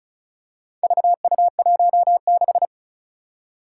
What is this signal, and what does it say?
Morse code, heard as a keyed tone, about 36 wpm, 700 Hz. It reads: VU16
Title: Four Challenging Letter-Number Combinations 35wpm